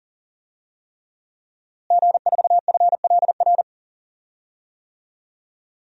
Morse code, heard as a keyed tone, about 40 wpm, 700 Hz. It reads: G4FLR
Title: DXCC-100 Call Signs - 5 Characters - Repeated 3x - 40wpm